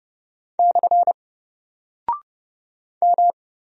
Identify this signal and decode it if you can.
Morse code, heard as a keyed tone, about 30 wpm, 700 Hz. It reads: / M
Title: Single Character (Letter:Number:Punctuation:Prosign) Alt 30wpm